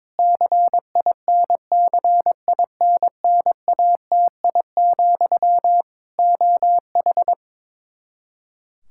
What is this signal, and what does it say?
Morse code, heard as a keyed tone, about 21 wpm, 700 Hz. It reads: CINCINNATI, OH
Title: K1USN SST QSO Practice 22wpm